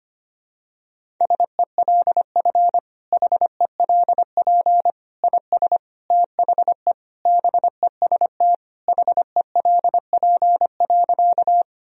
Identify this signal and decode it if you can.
Morse code, heard as a keyed tone, about 25 wpm, 700 Hz. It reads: SELF HELP IS THE BEST HELP.
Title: Sentences from Top 1000 Words - Review Part 1 - 25wpm